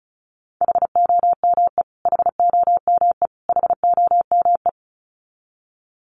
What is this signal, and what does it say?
Morse code, heard as a keyed tone, about 34 wpm, 700 Hz. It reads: HOME HOME HOME
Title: Top 300 Words Repeated 35wpm